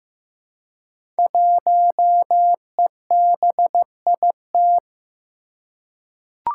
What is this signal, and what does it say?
Morse code, heard as a keyed tone, about 15 wpm, 700 Hz. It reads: 1EBIT
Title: Five Challenging Letter-Number Combinations 15wpm